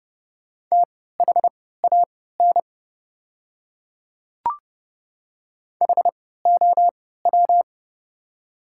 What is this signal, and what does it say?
Morse code, heard as a keyed tone, about 31 wpm, spaced as for 17 wpm, 690 Hz. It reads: THAN HOW
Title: 3x Character Spacing - Top 200 Words - 30wpm